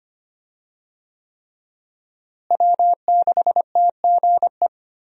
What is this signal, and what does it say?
Morse code, heard as a keyed tone, about 25 wpm, 700 Hz. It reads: W6TGE
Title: USA Call Signs - 5 Characters - Repeated 3x - 25wpm